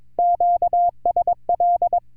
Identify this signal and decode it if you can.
Morse code, heard as a keyed tone, about 22 wpm, 700 Hz. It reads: QSL